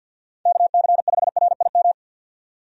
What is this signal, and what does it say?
Morse code, heard as a keyed tone, about 48 wpm, 700 Hz. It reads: KC5RIM
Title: USA Call Signs - 6 Characters - 50wpm